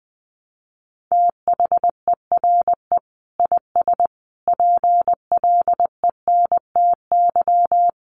Morse code, heard as a keyed tone, about 20 wpm, 700 Hz. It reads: THERE IS PLENTY
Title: Sentences from Top 1300 Words 20wpm